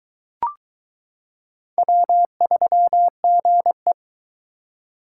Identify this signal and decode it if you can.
Morse code, heard as a keyed tone, about 22 wpm, 700 Hz. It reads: W3GE